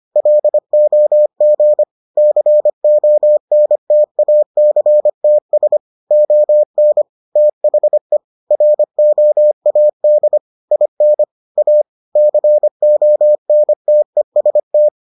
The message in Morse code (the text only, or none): LOG CONTACTS ON THE ROAD IN A CONTEST